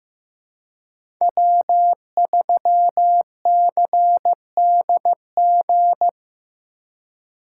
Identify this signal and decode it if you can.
Morse code, heard as a keyed tone, about 15 wpm, 700 Hz. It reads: W3CDG
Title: USA Call Signs - 2x Character Spacing - 15wpm